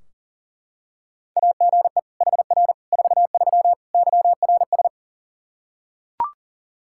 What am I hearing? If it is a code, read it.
Morse code, heard as a keyed tone, about 40 wpm, 700 Hz. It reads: AGE HR 43 YRS
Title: Practice QSO - Exchange 3 - More Info - 40wpm